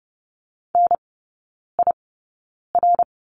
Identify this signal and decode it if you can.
Morse code, heard as a keyed tone, about 30 wpm, spaced as for 9 wpm, 700 Hz. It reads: NIR
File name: Lesson 16e - Letter c - Three Characters 30wpm